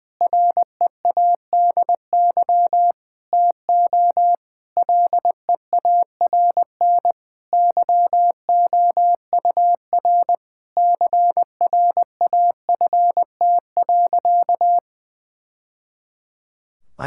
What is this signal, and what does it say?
Morse code, heard as a keyed tone, about 20 wpm, 700 Hz. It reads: READY TO LEARN YOUR CRAFT.
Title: Sentences from Top 1000 Words - Review Part 6 - 20wpm